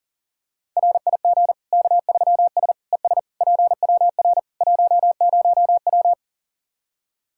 Morse code, heard as a keyed tone, about 40 wpm, 700 Hz. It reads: RIG K3S ES PWR 10W
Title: Practice QSO - Exchange 2 - Info - 40wpm